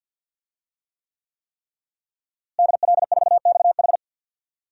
Morse code, heard as a keyed tone, about 46 wpm, 700 Hz. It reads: DL4XH
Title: DXCC-100 Call Signs - 5 Characters - Repeated 3x - 50wpm